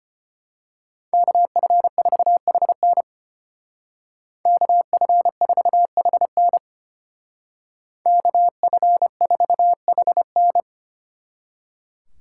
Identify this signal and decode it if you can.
Morse code, heard as a keyed tone, about 31 wpm, 700 Hz. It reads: KF4HN KF4HN KF4HN
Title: Speed Racing - USA Call Signs 25wpm